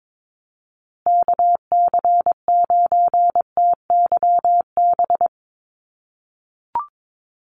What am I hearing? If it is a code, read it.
Morse code, heard as a keyed tone, about 22 wpm, 700 Hz. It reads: KC9TYB
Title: USA Call Signs Repeated 3x 22wpm